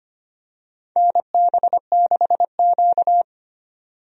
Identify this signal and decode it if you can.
Morse code, heard as a keyed tone, about 25 wpm, 700 Hz. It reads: NB6Q
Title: USA Call Signs - 4 Characters - 25wpm